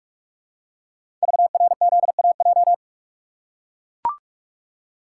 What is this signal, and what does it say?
Morse code, heard as a keyed tone, about 44 wpm, 700 Hz. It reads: VRZAJ
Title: Code Groups - 5 Letters - 45wpm